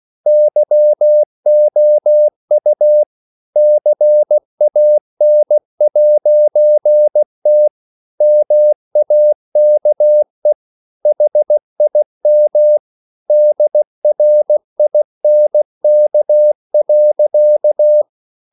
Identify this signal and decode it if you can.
Morse code, heard as a keyed tone, about 16 wpm, 600 Hz. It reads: YOU CAN'T MAKE HIM DRINK.
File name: CW / Practice / EnglishSayings16